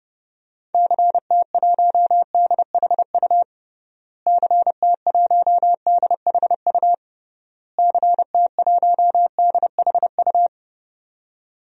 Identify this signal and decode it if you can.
Morse code, heard as a keyed tone, about 30 wpm, 700 Hz. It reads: CT1DHU CT1DHU CT1DHU
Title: DXCC-100 Call Signs - 6 Characters - Repeated 3x - 30wpm